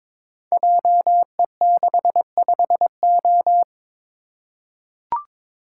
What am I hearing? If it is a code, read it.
Morse code, heard as a keyed tone, about 22 wpm, 700 Hz. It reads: JE65O
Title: Code Groups - 5 Letter:Numbers - 22wpm